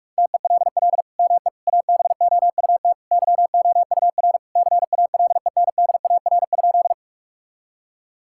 Morse code, heard as a keyed tone, about 46 wpm, 700 Hz. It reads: TELL ME ABOUT YOUR CALENDAR?
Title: Sentences from Top 1600 Words 45wpm